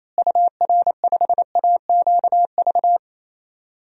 Morse code, heard as a keyed tone, about 28 wpm, 700 Hz. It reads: UR5AQV
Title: DXCC-100 Call Signs - 6 Characters - Repeated 2x - 28wpm